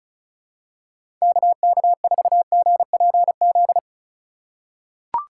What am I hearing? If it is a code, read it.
Morse code, heard as a keyed tone, about 36 wpm, 700 Hz. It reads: KK4GPZ
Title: USA Call Signs - 6 Characters - 3x Character Spacing - 35wpm